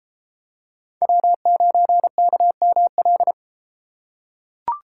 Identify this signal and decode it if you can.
Morse code, heard as a keyed tone, about 33 wpm, 700 Hz. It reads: W9KML